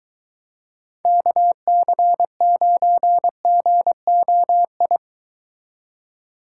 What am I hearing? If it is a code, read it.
Morse code, heard as a keyed tone, about 23 wpm, 700 Hz. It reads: KC9GOI